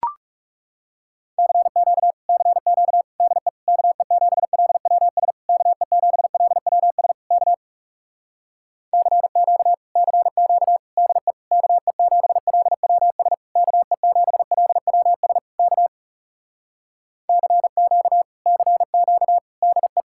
Morse code, heard as a keyed tone, about 38 wpm, 700 Hz. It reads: CQ CQ DE KE7LWS KE7LWS K CQ CQ DE KE7LWS KE7LWS K CQ CQ DE